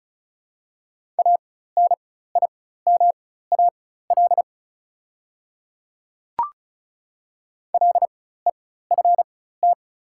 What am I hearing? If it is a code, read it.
Morse code, heard as a keyed tone, about 36 wpm, spaced as for 16 wpm, 690 Hz. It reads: ANIMAL LEFT
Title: 4x Character Spacing - Top 300 Words - 35wpm